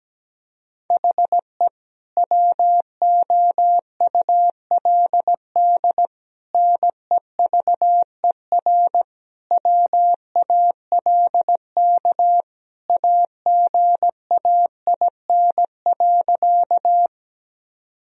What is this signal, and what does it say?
Morse code, heard as a keyed tone, about 17 wpm, 700 Hz. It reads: HE WOULD NEVER WALK AGAIN.
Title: Sentences from Top 1000 Words - Review Part 1 - 17wpm